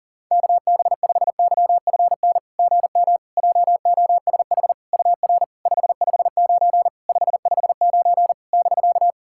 Morse code, heard as a keyed tone, about 41 wpm, 700 Hz. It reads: KB5YFN GM JOSH UR 559 559 <BK>
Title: SOTA QSO Practice 40wpm